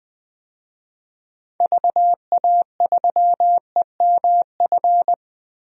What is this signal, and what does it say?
Morse code, heard as a keyed tone, about 20 wpm, 700 Hz. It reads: VA3EMF
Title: DXCC-100 Call Signs - 6 Characters - 20wpm